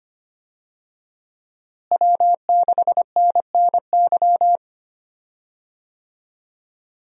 Morse code, heard as a keyed tone, about 25 wpm, 700 Hz. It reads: W6NNY
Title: USA Call Signs - 5 Characters - Repeated 3x - 25wpm